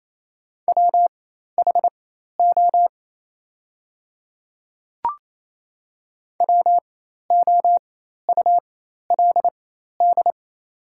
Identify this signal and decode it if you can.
Morse code, heard as a keyed tone, about 28 wpm, spaced as for 13 wpm, 700 Hz. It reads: WHO WOULD
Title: 4x Character Spacing - Top 100 Words - 28wpm